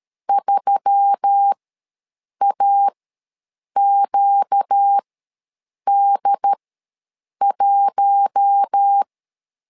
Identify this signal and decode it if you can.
Morse code, heard as a keyed tone, about 13 wpm, spaced as for 7 wpm, 800 Hz. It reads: 3AQD1
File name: LEZIONI / 4 Q1S3 ws 12wpm 2038